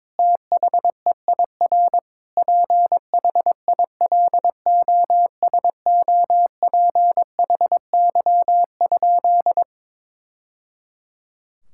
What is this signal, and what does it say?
Morse code, heard as a keyed tone, about 21 wpm, 700 Hz. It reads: THEIR PHILOSOPHY?
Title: Sentences from Top 1000 Words - Review Part 6 - 22wpm